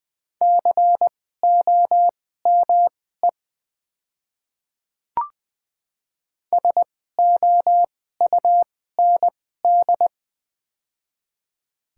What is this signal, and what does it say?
Morse code, heard as a keyed tone, about 20 wpm, spaced as for 15 wpm, 700 Hz. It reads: COME SOUND
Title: 2x Character Spacing - Top 100 Words - 20wpm